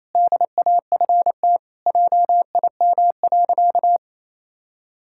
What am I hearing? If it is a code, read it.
Morse code, heard as a keyed tone, about 28 wpm, 700 Hz. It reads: DAFT JIM.
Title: Pangram Challenge v2 - 28wpm